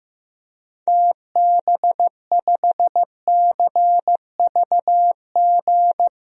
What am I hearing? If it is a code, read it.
Morse code, heard as a keyed tone, about 15 wpm, 700 Hz. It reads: TB5CVG
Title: DXCC-100 Call Signs - 6 Characters - 15wpm